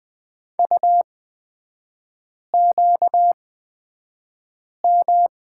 Morse code, heard as a keyed tone, about 20 wpm, spaced as for 5 wpm, 700 Hz. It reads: UQM
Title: 250 Lesson 40e - BK - Three Characters 20wpm